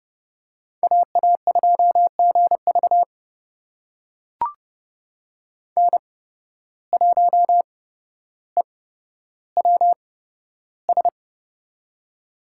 Morse code, heard as a keyed tone, about 29 wpm, spaced as for 13 wpm, 700 Hz. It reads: AA2GV N1EWS
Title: USA Call Signs - 5 Characters - 8x Character Spacing - 30wpm